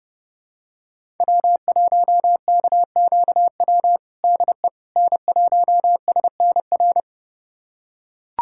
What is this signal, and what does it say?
Morse code, heard as a keyed tone, about 30 wpm, 700 Hz. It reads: W1KQW DE N1SNR
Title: USA Callsign DE Callsign - 5 Characters - 30wpm